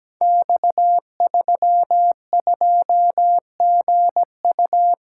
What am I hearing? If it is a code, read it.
Morse code, heard as a keyed tone, about 17 wpm, 700 Hz. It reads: X32GU
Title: Code Groups - 5 Letter:Numbers - 17wpm